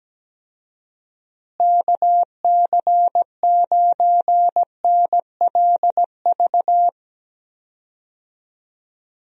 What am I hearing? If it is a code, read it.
Morse code, heard as a keyed tone, about 17 wpm, 700 Hz. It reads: KC9NLV
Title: USA Call Signs - 6 Characters - Repeated 3x - 17wpm